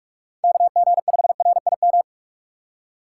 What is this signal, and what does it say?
Morse code, heard as a keyed tone, about 46 wpm, 700 Hz. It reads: KG5RIM
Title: USA Call Signs - 6 Characters - 6x Character Spacing - 45wpm